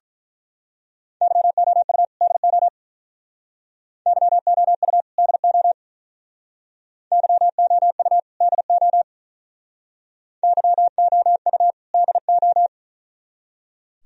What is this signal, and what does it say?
Morse code, heard as a keyed tone, about 43 wpm, 700 Hz. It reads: YOU DO YOU DO YOU DO YOU DO
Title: Speed-Racing - Sets of 2 Words 35wpm